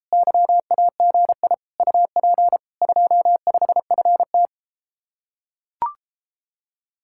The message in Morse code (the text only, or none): YAGI UP 25FT